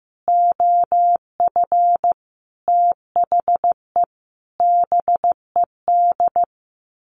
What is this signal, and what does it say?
Morse code, heard as a keyed tone, about 15 wpm, 700 Hz. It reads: OF THE BED
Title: Sets of 5 Words Encore2 15wpm